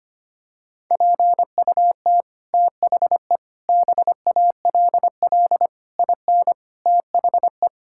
Morse code, heard as a keyed tone, about 25 wpm, 700 Hz. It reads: PUT THE BALL IN THE
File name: Sets of 9 Words - 25wpm